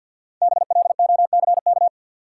Morse code, heard as a keyed tone, about 51 wpm, 700 Hz. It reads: BLOCK
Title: Sentences from Top 1600 Words - New Word Drill - 50wpm